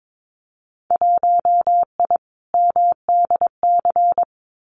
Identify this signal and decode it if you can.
Morse code, heard as a keyed tone, about 22 wpm, 700 Hz. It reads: 1I MDC
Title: Winter Field Day Exchanges 22wpm